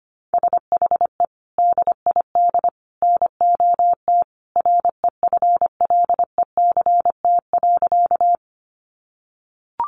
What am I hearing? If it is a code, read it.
Morse code, heard as a keyed tone, about 25 wpm, 700 Hz. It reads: SHE DID NOT REFLECT.
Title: Sentences from Top 1500 Words 25wpm